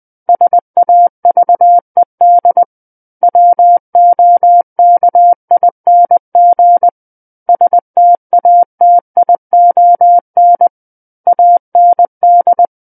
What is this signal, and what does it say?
Morse code, heard as a keyed tone, about 20 wpm, 700 Hz. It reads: SAVED WOKING STATION AND
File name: 20 WPM